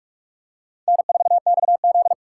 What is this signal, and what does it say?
Morse code, heard as a keyed tone, about 44 wpm, 700 Hz. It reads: N4XZ